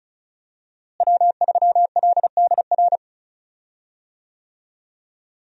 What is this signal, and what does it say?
Morse code, heard as a keyed tone, about 36 wpm, 700 Hz. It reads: W3LDR
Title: USA Call Signs - 5 Characters - Repeated 3x - 35wpm